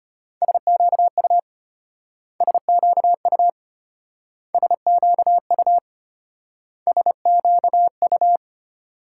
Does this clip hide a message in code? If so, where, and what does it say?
Yes: Morse code from 0.4 to 8.4 s, heard as a keyed tone, about 32 wpm, 700 Hz. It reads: SQU SQU SQU SQU